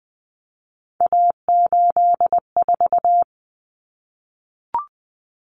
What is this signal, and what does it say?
Morse code, heard as a keyed tone, about 20 wpm, 700 Hz. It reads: A84
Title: Common Cut Numbers - 3-Digit 20wpm